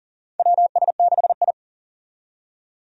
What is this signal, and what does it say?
Morse code, heard as a keyed tone, about 40 wpm, 700 Hz. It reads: WS6I
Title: USA Call Signs - 4 Characters - 40wpm